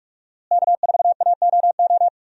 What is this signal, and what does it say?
Morse code, heard as a keyed tone, about 42 wpm, 700 Hz. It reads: K4AOO
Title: USA Call Signs - 5 Characters - Repeated 3x - 45wpm